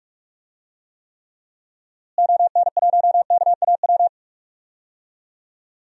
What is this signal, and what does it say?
Morse code, heard as a keyed tone, about 42 wpm, 700 Hz. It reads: ON1KAW
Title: DXCC-100 Call Signs - 6 Characters - Repeated 3x - 45wpm